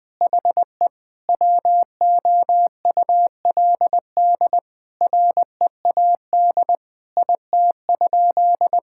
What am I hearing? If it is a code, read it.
Morse code, heard as a keyed tone, about 19 wpm, 700 Hz. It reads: HE WOULD READ IT?
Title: Sentences from Top 1600 Words 20wpm